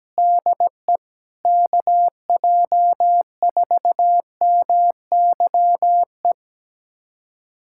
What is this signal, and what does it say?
Morse code, heard as a keyed tone, about 17 wpm, 700 Hz. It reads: DE KJ4MYE
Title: USA Callsign DE Callsign - 6 Characters - 17wpm